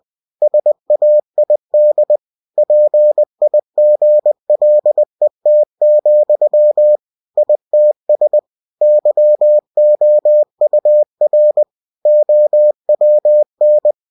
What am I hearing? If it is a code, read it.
Morse code, heard as a keyed tone, about 20 wpm, 600 Hz. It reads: SAID PIGLET, ITS YOUR OWN
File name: cw-books / winnie-the-pooh / winnie-20wpm-page-0049